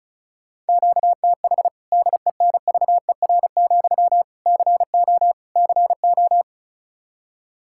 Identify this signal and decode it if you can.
Morse code, heard as a keyed tone, about 36 wpm, 700 Hz. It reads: QTH DENVER, CO CO
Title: Practice QSO - Exchange 1 - Beginning - 35wpm